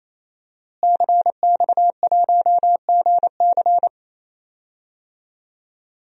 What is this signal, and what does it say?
Morse code, heard as a keyed tone, about 28 wpm, 700 Hz. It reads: CX1GC
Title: DXCC-100 Call Signs - 5 Characters - Repeated 3x - 28wpm